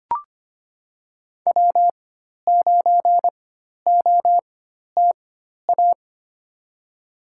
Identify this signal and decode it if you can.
Morse code, heard as a keyed tone, about 25 wpm, spaced as for 12 wpm, 700 Hz. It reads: W9OTA
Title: USA Call Signs - 5 Characters - 4x Character Spacing - 25wpm